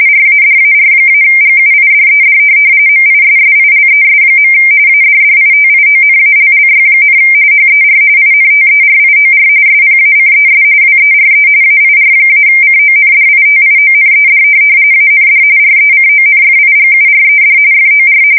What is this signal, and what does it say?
RTTY, about 45.45 baud. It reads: RADAR-GUIDED MISSILES. ADDITIONALLY, HER INTERIOR SPACES WERE AIR-CONDITIONED AND RECONFIGURED FOR
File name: fsk170 / USS-New-Jersey